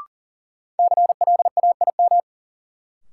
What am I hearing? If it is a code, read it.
Morse code, heard as a keyed tone, about 38 wpm, 700 Hz. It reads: CLAIM
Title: Sentences from Top 1500 Words - New Word Drill Alt - 40wpm